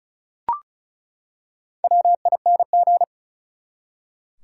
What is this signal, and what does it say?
Morse code, heard as a keyed tone, about 35 wpm, 700 Hz. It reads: WING
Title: Four Letter Words - 35wpm